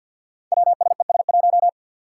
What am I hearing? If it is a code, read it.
Morse code, heard as a keyed tone, about 48 wpm, 700 Hz. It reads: WSES1
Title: Five Challenging Letter-Number Combinations 50wpm